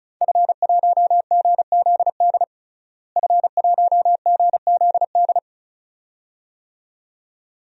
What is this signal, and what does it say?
Morse code, heard as a keyed tone, about 34 wpm, 700 Hz. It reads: F1GZD F1GZD
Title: DXCC-100 Call Signs - 5 Characters - Repeated 2x - 35wpm